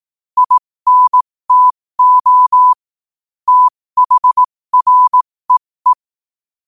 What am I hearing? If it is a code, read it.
Morse code, heard as a keyed tone, about 18 wpm, spaced as for 15 wpm, 1000 Hz. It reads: INTO THREE
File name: qotdfiles / 15wpm